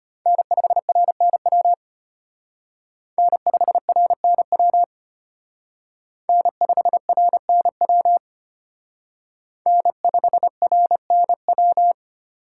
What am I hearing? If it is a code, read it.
Morse code, heard as a keyed tone, about 32 wpm, 700 Hz. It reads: N5RNW N5RNW N5RNW N5RNW